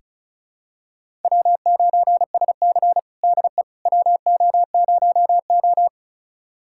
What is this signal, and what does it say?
Morse code, heard as a keyed tone, about 36 wpm, 700 Hz. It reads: W9SC DE WO0O
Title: USA Callsign DE Callsign - 4 Characters - 35wpm